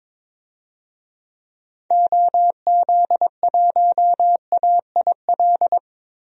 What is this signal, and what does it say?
Morse code, heard as a keyed tone, about 22 wpm, 700 Hz. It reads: OZ1AIL
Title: DXCC-100 Call Signs - 6 Characters - 22wpm